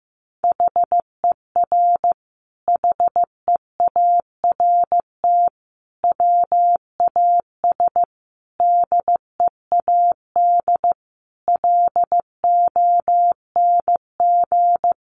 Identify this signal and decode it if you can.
Morse code, heard as a keyed tone, about 15 wpm, 700 Hz. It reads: HER HEART WAS DEAD LONG
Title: Sentences from Top 1500 Words 15wpm